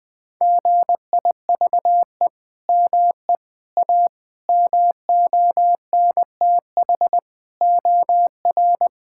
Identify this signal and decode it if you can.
Morse code, heard as a keyed tone, about 20 wpm, 700 Hz. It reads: GIVE ME A MONTH OR
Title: Sentences from Top 1000 Words - Review Part 5 - 20wpm